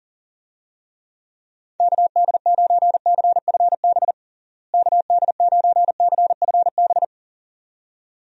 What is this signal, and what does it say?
Morse code, heard as a keyed tone, about 37 wpm, 700 Hz. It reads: KD9CFB KD9CFB
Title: USA Call Signs - 6 Characters - Repeated 2x - 40wpm